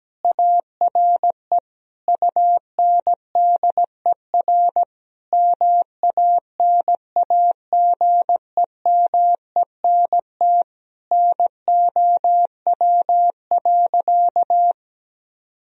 Morse code, heard as a keyed tone, about 17 wpm, 700 Hz. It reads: ARE UNDER MANAGEMENT NOW.
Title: Sentences from Top 1000 Words - Review Part 4 - 17wpm